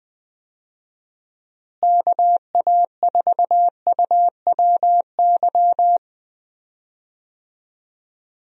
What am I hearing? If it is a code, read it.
Morse code, heard as a keyed tone, about 21 wpm, 700 Hz. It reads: KA4UWY
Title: USA Call Signs - 6 Characters - Repeated 3x - 20wpm